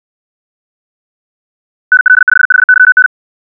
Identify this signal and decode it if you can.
Morse code, heard as a keyed tone, about 64 wpm, 1500 Hz. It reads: NL5HYN
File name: ja7uhv / SampleHST3